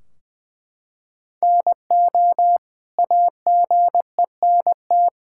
Morse code, heard as a keyed tone, about 20 wpm, 700 Hz. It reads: NO AGENT
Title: Sentences from Top 1600 Words 20wpm